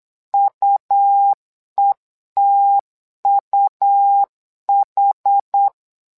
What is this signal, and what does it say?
Morse code, heard as a keyed tone, about 8 wpm, 800 Hz. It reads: UETUH
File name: LEZIONI / 2 UHDN / 2 UH 8wpm 2129